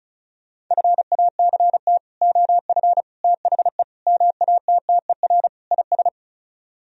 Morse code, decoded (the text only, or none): FACT OF THE MATTER IS